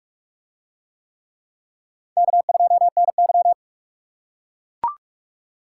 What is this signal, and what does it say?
Morse code, heard as a keyed tone, about 44 wpm, 700 Hz. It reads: K2NY